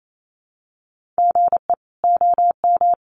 Morse code, heard as a keyed tone, about 28 wpm, 700 Hz. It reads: GE OM
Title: Common QSO Phrases 28wpm